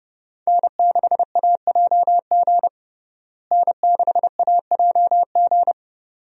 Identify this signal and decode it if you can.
Morse code, heard as a keyed tone, about 32 wpm, 700 Hz. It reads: N6AJG N6AJG
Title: USA Call Signs - 5 Characters - Repeated 2x - 30wpm